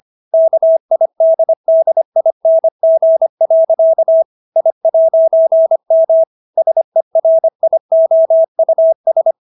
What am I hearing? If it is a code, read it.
Morse code, heard as a keyed tone, about 25 wpm, 650 Hz. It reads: KIDDING. I'M SERIOUS